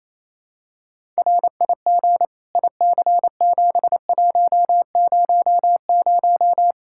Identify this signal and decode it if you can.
Morse code, heard as a keyed tone, about 28 wpm, 700 Hz. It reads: RIG IC7100
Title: Practice QSO - Exchange 2 - Info - 28wpm